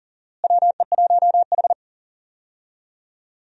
Morse code, heard as a keyed tone, about 40 wpm, 700 Hz. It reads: WE1H
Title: USA Call Signs - 4 Characters - Repeated 3x - 40wpm